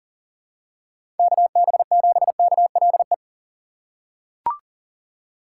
Morse code, heard as a keyed tone, about 37 wpm, 700 Hz. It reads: KB7KLE
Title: USA Call Signs - 4x Character Spacing - 40wpm